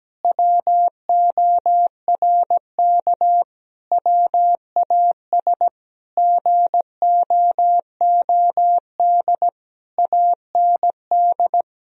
Morse code, heard as a keyed tone, about 17 wpm, 700 Hz. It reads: WORK WAS GOOD AND
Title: Sentences from Top 1000 Words - Review Part 2 - 17wpm